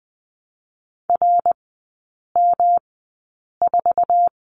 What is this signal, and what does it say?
Morse code, heard as a keyed tone, about 20 wpm, spaced as for 8 wpm, 700 Hz. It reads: RM4
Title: 107 Lesson 18e - Letter m - Three Characters 20wpm